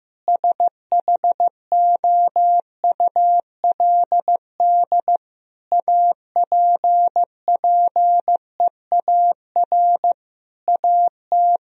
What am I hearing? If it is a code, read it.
Morse code, heard as a keyed tone, about 16 wpm, 700 Hz. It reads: SHOULD APPEAR AT
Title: Sentences from Top 1000 Words - Review Part 2 - 15wpm